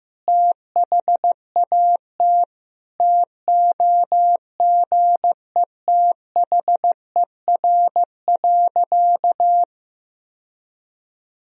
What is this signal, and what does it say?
Morse code, heard as a keyed tone, about 16 wpm, 700 Hz. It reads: THAT TOGETHER.
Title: Sentences from Top 1000 Words - Review Part 1 - 15wpm